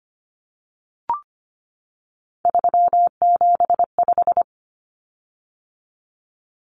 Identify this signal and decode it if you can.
Morse code, heard as a keyed tone, about 25 wpm, 700 Hz. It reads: 375
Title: Common Cut Numbers - 3-Digit 25wpm